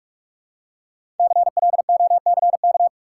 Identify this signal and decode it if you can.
Morse code, heard as a keyed tone, about 44 wpm, 700 Hz. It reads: CLOCK
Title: Five Letter Words - 45wpm